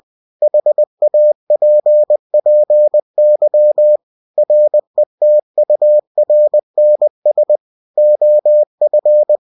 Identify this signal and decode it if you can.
Morse code, heard as a keyed tone, about 20 wpm, 600 Hz. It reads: HAPPY RETURNS OF
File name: cw-books / winnie-the-pooh / winnie-20wpm-page-0050